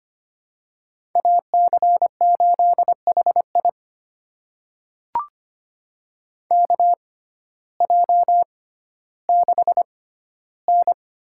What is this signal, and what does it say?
Morse code, heard as a keyed tone, about 25 wpm, spaced as for 14 wpm, 700 Hz. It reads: AC8HI KJ6N
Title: USA Call Signs - 6x Character Spacing - 25wpm